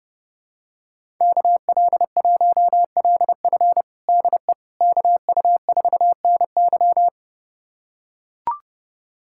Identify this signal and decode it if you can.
Morse code, heard as a keyed tone, about 30 wpm, 700 Hz. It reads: KL1LF DE KU4NY
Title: USA Callsign DE Callsign - 5 Characters - 30wpm